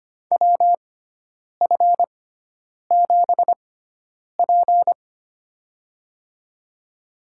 Morse code, heard as a keyed tone, about 25 wpm, spaced as for 9 wpm, 700 Hz. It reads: WF7P
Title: USA Call Signs - 4 Characters - 6x Character Spacing - 25wpm